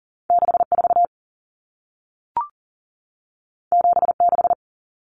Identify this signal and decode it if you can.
Morse code, heard as a keyed tone, about 40 wpm, 700 Hz. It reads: 64 76
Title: Numbers - 2-Digit 40wpm